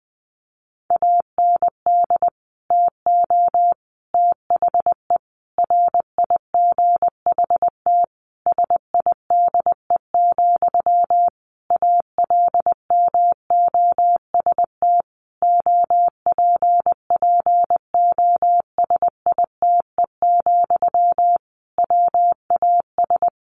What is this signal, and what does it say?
Morse code, heard as a keyed tone, about 20 wpm, 700 Hz. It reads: AND TO THE RIGHT SIDE, ALMOST OPPOSITE, WAS
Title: Sentences from Top 1300 Words 20wpm